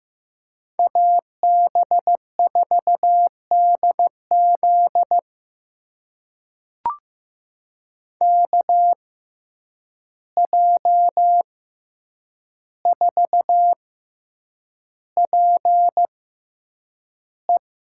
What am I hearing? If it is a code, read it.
Morse code, heard as a keyed tone, about 15 wpm, spaced as for 8 wpm, 700 Hz. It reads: AB4DZ KJ4PE
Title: USA Call Signs - 5 Characters - 6x Character Spacing - 15wpm